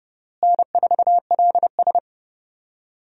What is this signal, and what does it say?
Morse code, heard as a keyed tone, about 31 wpm, 700 Hz. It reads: N4LS
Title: USA Call Signs - 4 Characters - Repeated 3x - 30wpm